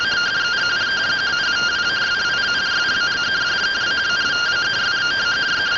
Two-tone teleprinter signal, about 170 Hz shift.